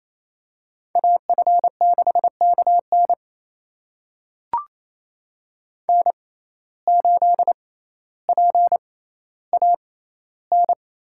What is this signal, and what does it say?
Morse code, heard as a keyed tone, about 28 wpm, spaced as for 14 wpm, 700 Hz. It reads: AF6KN N8PAN
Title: USA Call Signs - 5 Characters - 6x Character Spacing - 28wpm